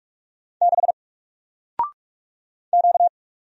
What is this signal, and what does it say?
Morse code, heard as a keyed tone, about 46 wpm, 700 Hz. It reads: 6 Q
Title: Single Letter-Number 45wpm